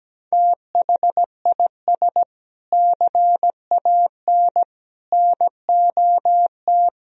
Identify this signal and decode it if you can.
Morse code, heard as a keyed tone, about 17 wpm, 700 Hz. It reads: THIS CAN NOT
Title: Sentences from Top 1000 Words - Review Part 3 - 17wpm